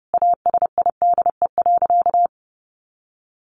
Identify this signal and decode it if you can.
Morse code, heard as a keyed tone, about 30 wpm, 700 Hz. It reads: ASIDE.
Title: A Sound Of Thunder 30wpm